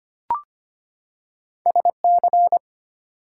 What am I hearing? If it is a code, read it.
Morse code, heard as a keyed tone, about 25 wpm, 700 Hz. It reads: SC
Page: morse-code-ninja-practice-25wpm